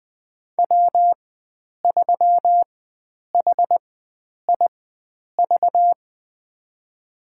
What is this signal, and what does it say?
Morse code, heard as a keyed tone, about 21 wpm, spaced as for 9 wpm, 700 Hz. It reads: W3HIV
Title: USA Call Signs - 5 Characters - 4x Character Spacing - 20wpm